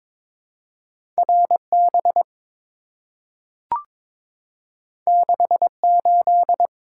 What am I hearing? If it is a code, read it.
Morse code, heard as a keyed tone, about 22 wpm, 700 Hz. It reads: RB 68
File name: practice / 22wpm / Two Letter-Number Combinations - 22wpm